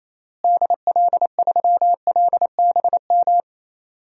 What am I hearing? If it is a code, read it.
Morse code, heard as a keyed tone, about 28 wpm, 700 Hz. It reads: DL3LBM
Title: DXCC-100 Call Signs - 6 Characters - Repeated 3x - 28wpm